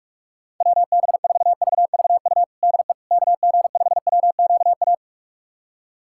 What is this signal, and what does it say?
Morse code, heard as a keyed tone, about 46 wpm, 700 Hz. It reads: WB4VVU DE KG5WQA
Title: USA Callsign DE Callsign - 6 Characters - 45wpm